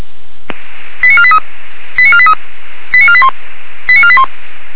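Five-tone selective call (ZVEI), heard as quick runs of five tones, tones of about 70 ms. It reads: 89483 89483 89481 89481
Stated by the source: ZVEI five-tone call received via SDR.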